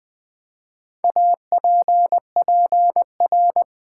Morse code, heard as a keyed tone, about 20 wpm, 700 Hz. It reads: APPR
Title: Build A Word - 4 Letters At A Time - 12-Letter Words - 20wpm